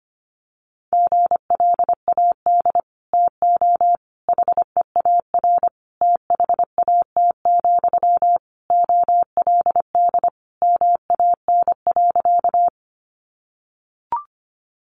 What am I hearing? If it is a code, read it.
Morse code, heard as a keyed tone, about 25 wpm, 700 Hz. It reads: GLAD TO HEAR THAT, OLD MAN.
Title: Sentences from Top 800 Words 25wpm